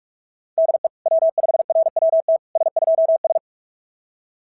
Morse code, heard as a keyed tone, about 44 wpm, 660 Hz. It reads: DE W5RWT S2S
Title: SOTA QSO Practice 45wpm